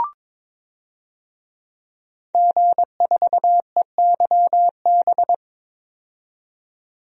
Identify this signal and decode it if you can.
Morse code, heard as a keyed tone, about 22 wpm, 700 Hz. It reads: G4EYB
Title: DXCC-100 Call Signs - 5 Characters - 22wpm